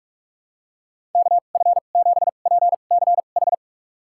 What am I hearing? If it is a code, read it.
Morse code, heard as a keyed tone, about 46 wpm, spaced as for 32 wpm, 700 Hz. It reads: KF7PCH
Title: USA Call Signs - 2x Character Spacing - 45wpm